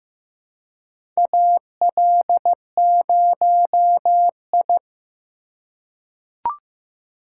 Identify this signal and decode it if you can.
Morse code, heard as a keyed tone, about 15 wpm, 700 Hz. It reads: AL0I
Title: USA Call Signs - 4 Characters - 3x Character Spacing - 15wpm